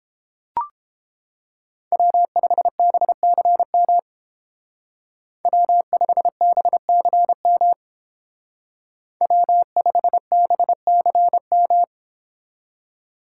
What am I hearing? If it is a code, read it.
Morse code, heard as a keyed tone, about 28 wpm, 700 Hz. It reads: W5BCM W5BCM W5BCM